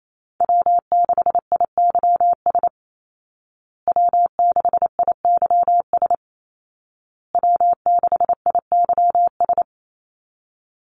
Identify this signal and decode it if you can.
Morse code, heard as a keyed tone, about 28 wpm, 700 Hz. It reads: W6IYS W6IYS W6IYS
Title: Call Signs Repeated 2x - 28wpm